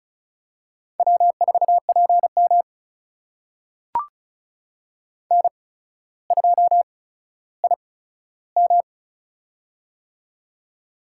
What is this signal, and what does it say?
Morse code, heard as a keyed tone, about 36 wpm, spaced as for 15 wpm, 700 Hz. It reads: W4PM N2IM
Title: USA Call Signs - 4 Characters - 8x Character Spacing - 35wpm